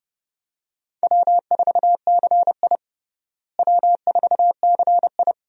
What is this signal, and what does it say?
Morse code, heard as a keyed tone, about 30 wpm, 700 Hz. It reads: W4CI W4CI
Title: USA Call Signs - 4 Characters - Repeated 2x - 30wpm